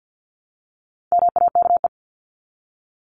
Morse code, heard as a keyed tone, about 50 wpm, 700 Hz. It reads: MAKE
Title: Top 300 Words Repeated 50wpm